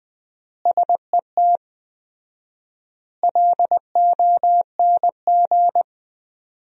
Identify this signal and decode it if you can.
Morse code, heard as a keyed tone, about 20 wpm, 700 Hz. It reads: SET LONG
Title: 4x Word-Spacing - Sentences from Top 300 Words - 20wpm